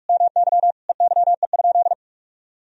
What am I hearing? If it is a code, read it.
Morse code, heard as a keyed tone, about 46 wpm, 700 Hz. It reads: MY EYE?
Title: Sentences from Top 500 Words and Abbreviations 45wpm